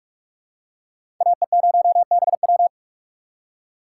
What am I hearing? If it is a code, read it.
Morse code, heard as a keyed tone, about 46 wpm, 700 Hz. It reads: AE0BW
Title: USA Call Signs - 4x Character Spacing - 45wpm